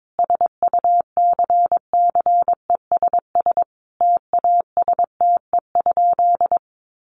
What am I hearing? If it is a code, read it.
Morse code, heard as a keyed tone, about 22 wpm, 700 Hz. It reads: SUCCESS TASTE?
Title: Sentences from Top 1100 Words 22wpm